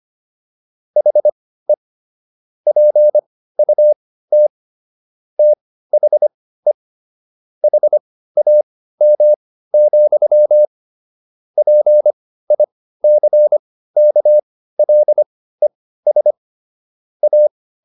Morse code, heard as a keyed tone, about 25 wpm, spaced as for 15 wpm, 600 Hz. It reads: HE PUT THE HAM, PICKLES A